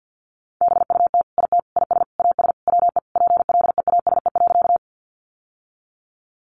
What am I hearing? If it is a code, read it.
Morse code, heard as a keyed tone, about 50 wpm, 700 Hz. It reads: BUT IT IS AS WE PLEASE.
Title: Sentences from Top 1500 Words 50wpm